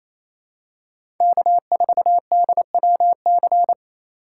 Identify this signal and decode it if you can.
Morse code, heard as a keyed tone, about 28 wpm, 700 Hz. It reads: K4DWC
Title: USA Call Signs - 3x Character Spacing - 28wpm